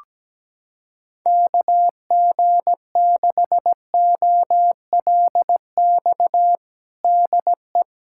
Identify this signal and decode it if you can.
Morse code, heard as a keyed tone, about 17 wpm, 700 Hz. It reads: KG6OLX DE
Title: USA Callsign DE Callsign - 6 Characters - 17wpm